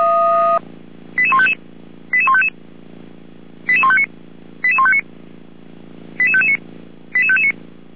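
Five-tone selective call (ZVEI), heard as quick runs of five tones, tones of about 70 ms. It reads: 88155 88155 88159 88159 88559 88559
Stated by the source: ZVEI five-tone selective call 88419, received via SDR.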